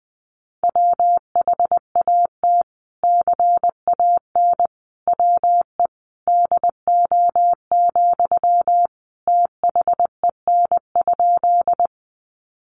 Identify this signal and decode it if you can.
Morse code, heard as a keyed tone, about 19 wpm, 700 Hz. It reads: WHAT CAN WE DO, THEN?
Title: Sentences from Top 100 Words 20wpm